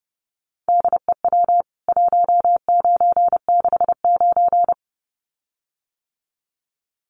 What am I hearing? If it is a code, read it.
Morse code, heard as a keyed tone, about 28 wpm, 700 Hz. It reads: DEW 1969
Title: CWT Exchange 30wpm